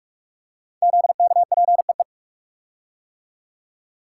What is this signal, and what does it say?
Morse code, heard as a keyed tone, about 44 wpm, 690 Hz. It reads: ZKPEE
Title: Code Groups - 5 Letters - 45wpm